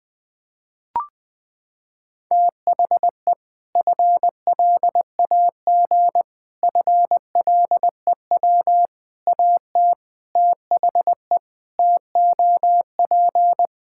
Morse code, heard as a keyed tone, about 20 wpm, 700 Hz. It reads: THE FLAG FLEW AT THE TOP
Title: Sentences with 4-Word Phrases - 20wpm